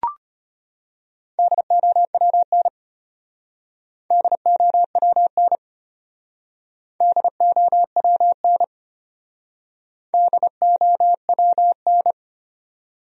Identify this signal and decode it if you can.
Morse code, heard as a keyed tone, about 32 wpm, 700 Hz. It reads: DOWN DOWN DOWN DOWN